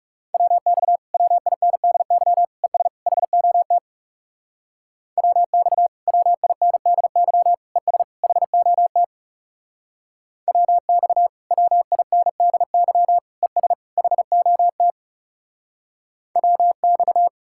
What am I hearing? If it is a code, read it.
Morse code, heard as a keyed tone, about 36 wpm, 700 Hz. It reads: WX WINDY ES HOT WX WINDY ES HOT WX WINDY ES HOT WX
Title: Speed-Racing Common QSO Phrases - 30wpm